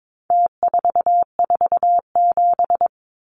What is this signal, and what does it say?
Morse code, heard as a keyed tone, about 22 wpm, 700 Hz. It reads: T447
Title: Numbers - Contest Style - 4-Digit 22wpm